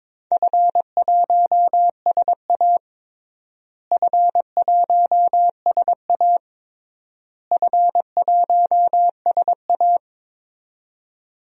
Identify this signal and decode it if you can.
Morse code, heard as a keyed tone, about 22 wpm, 700 Hz. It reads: F1SA F1SA F1SA
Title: DXCC-100 Call Signs - 4 Characters - Repeated 3x - 22wpm